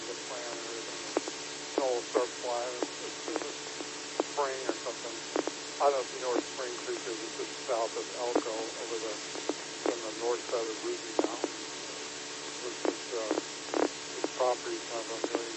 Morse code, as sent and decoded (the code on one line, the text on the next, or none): none